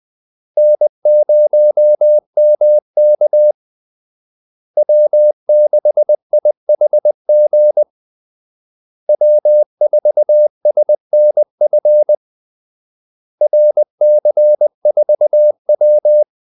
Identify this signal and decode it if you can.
Morse code, heard as a keyed tone, about 20 wpm, 600 Hz. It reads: N0MK W6IHG W4SNF RC4W
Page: Kaldesignaler 20 wpm